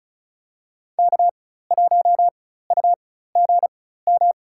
Morse code, heard as a keyed tone, about 36 wpm, spaced as for 16 wpm, 700 Hz. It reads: K1UGM
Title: USA Call Signs - 5 Characters - 4x Character Spacing - 35wpm